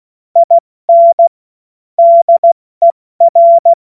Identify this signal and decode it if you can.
Morse code, heard as a keyed tone, about 16 wpm, 680 Hz. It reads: IN DER
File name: Morsekurs / Witze / Witz 05. April 2022 70 F80